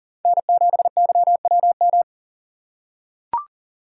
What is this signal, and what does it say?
Morse code, heard as a keyed tone, about 39 wpm, 700 Hz. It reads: N7YWM
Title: USA Call Signs - 5 Characters - 2x Character Spacing - 40wpm